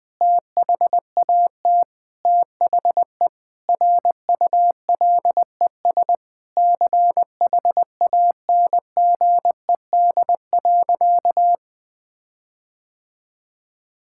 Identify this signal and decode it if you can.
Morse code, heard as a keyed tone, about 20 wpm, 700 Hz. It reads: THAT THE RULES CHANGED.
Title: Sentences with 7-Word Phrases - Spoken Beginning - 20wpm